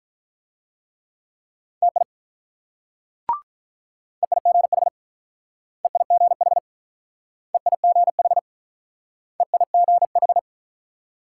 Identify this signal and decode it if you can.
Morse code, heard as a keyed tone, about 44 wpm, 700 Hz. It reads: TI EIGH EIGH EIGH EIGH
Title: Speed-Racing Multi-letter Phonemes - 35wpm